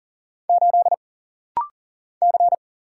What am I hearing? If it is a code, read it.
Morse code, heard as a keyed tone, about 40 wpm, 700 Hz. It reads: 8 C
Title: Single Letter-Number - Rapid-Fire 40wpm